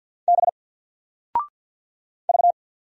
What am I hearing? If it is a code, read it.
Morse code, heard as a keyed tone, about 53 wpm, 700 Hz. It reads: B V
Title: Single Letter-Number 50wpm